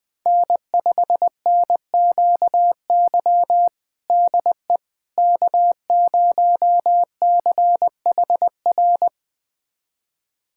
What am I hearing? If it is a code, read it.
Morse code, heard as a keyed tone, about 20 wpm, 700 Hz. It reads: N5NQY DE K0CHR
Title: USA Callsign DE Callsign - 5 Characters - 20wpm